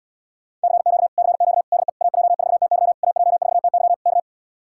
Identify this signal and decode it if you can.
Morse code, heard as a keyed tone, about 80 wpm, 700 Hz. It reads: CQ CQ DE AJ4TY AJ4TY K